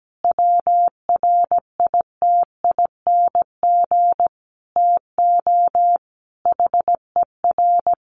Morse code, heard as a keyed tone, about 17 wpm, 700 Hz. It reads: WRITING TO HER
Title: Sentences from Top 800 Words 17wpm